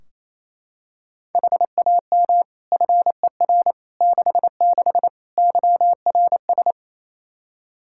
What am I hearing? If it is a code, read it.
Morse code, heard as a keyed tone, about 28 wpm, 700 Hz. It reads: HAM FER 66 YRS
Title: Practice QSO - Exchange 3 - More Info - 28wpm